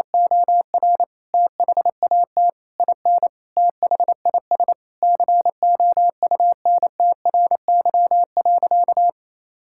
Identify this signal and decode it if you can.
Morse code, heard as a keyed tone, about 28 wpm, 700 Hz. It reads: OR THAT IN THIS COUNTRY.
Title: Sentences from Top 1000 Words - Review Part 5 - 28wpm